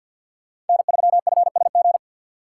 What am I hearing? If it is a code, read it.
Morse code, heard as a keyed tone, about 51 wpm, 700 Hz. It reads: N3FSG
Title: USA Call Signs - 5 Characters - 2x Character Spacing - 50wpm